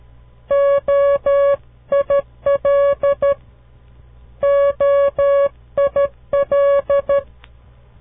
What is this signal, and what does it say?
Morse code, heard as a keyed tone, about 13 wpm, 560 Hz. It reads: OIL OIL
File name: CW Learning Programs / farnsworth / unit 2